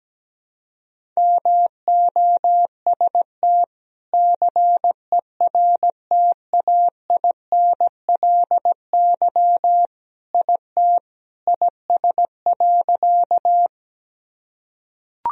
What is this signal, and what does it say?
Morse code, heard as a keyed tone, about 17 wpm, 700 Hz. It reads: MOST CERTAINLY IT IS.
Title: Sentences from Top 1000 Words - Review Part 1 - 17wpm